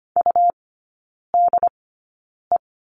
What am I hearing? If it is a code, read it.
Morse code, heard as a keyed tone, about 25 wpm, spaced as for 9 wpm, 700 Hz. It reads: UDE
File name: Lesson 17e - Letter u - Three Characters 25wpm